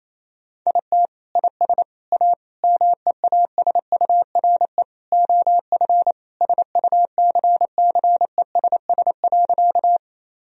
Morse code, heard as a keyed tone, about 28 wpm, 700 Hz. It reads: IT IS A MEASURE OF SUCCESS.
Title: Sentences from Top 1000 Words - Review Part 2 - 28wpm